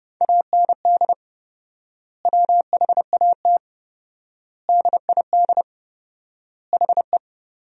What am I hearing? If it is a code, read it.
Morse code, heard as a keyed tone, about 30 wpm, 700 Hz. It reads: AND WHAT DID HE
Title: 4x Word-Spacing - Sentences from Top 200 Words - 30wpm